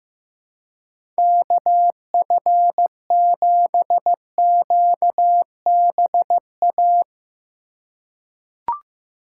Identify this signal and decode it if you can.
Morse code, heard as a keyed tone, about 15 wpm, 700 Hz. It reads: KF7QBA
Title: USA Call Signs - 6 Characters - 2x Character Spacing - 15wpm